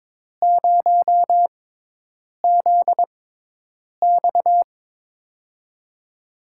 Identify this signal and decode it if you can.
Morse code, heard as a keyed tone, about 22 wpm, spaced as for 8 wpm, 700 Hz. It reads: 0ZX
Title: USA Call Signs - 4 Characters - 6x Character Spacing - 22wpm